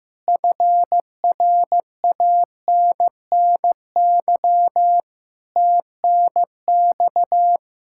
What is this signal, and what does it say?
Morse code, heard as a keyed tone, about 15 wpm, 700 Hz. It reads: FRANNY TNX
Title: Practice QSO - Exchange 4 - Ending - 15wpm